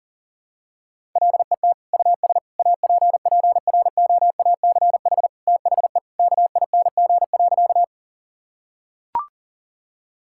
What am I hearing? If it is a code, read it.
Morse code, heard as a keyed tone, about 40 wpm, 700 Hz. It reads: LET US APPROACH THE KING.
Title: Sentences from Top 1000 Words - Review Part 6 - 40wpm